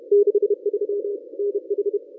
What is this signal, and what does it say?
Morse code, heard as a keyed tone, about 31 wpm, 410 Hz. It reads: 63 NH
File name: contests / ss-cw-k5tr-op-n5rz -1000